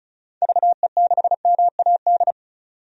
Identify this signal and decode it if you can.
Morse code, heard as a keyed tone, about 34 wpm, 700 Hz. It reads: VE6MAD
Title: DXCC-100 Call Signs - 6 Characters - 35wpm